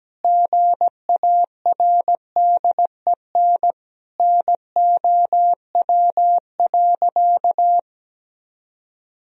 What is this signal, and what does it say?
Morse code, heard as a keyed tone, about 17 wpm, 700 Hz. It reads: GARDEN NOW.
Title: Sentences from Top 1000 Words - Review Part 4 - 17wpm